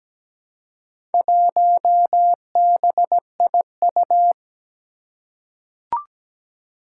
Morse code, heard as a keyed tone, about 17 wpm, 700 Hz. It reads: 1BIU
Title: Four Challenging Letter-Number Combinations 17wpm